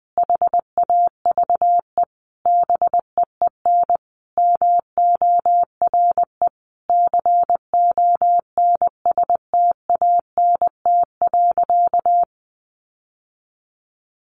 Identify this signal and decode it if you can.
Morse code, heard as a keyed tone, about 20 wpm, 700 Hz. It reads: HAVE BEEN MORE CONSTANT.
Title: Sentences from Top 1500 Words 20wpm